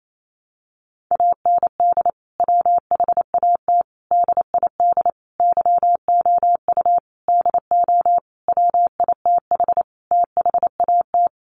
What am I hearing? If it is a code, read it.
Morse code, heard as a keyed tone, about 28 wpm, 700 Hz. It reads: AND WHAT DID YOU DO WITH THAT
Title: Sentences from Top 300 Words 28wpm